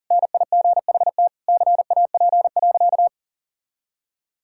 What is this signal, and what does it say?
Morse code, heard as a keyed tone, about 40 wpm, 700 Hz. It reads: NIGHT CAP.
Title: Sentences from Top 1600 Words 40wpm